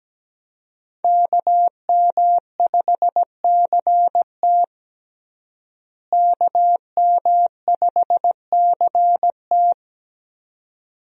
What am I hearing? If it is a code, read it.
Morse code, heard as a keyed tone, about 17 wpm, 700 Hz. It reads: KM5CT KM5CT
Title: USA Call Signs - 5 Characters - Repeated 2x - 17wpm